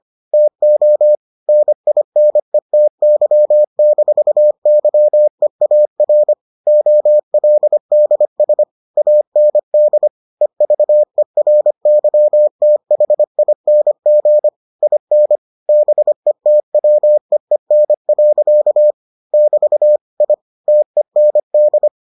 Morse code, heard as a keyed tone, about 25 wpm, 600 Hz. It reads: TO NINETY-YEAR OLDS AND EVERYTHING IN BETWEEN. <BT> I TEND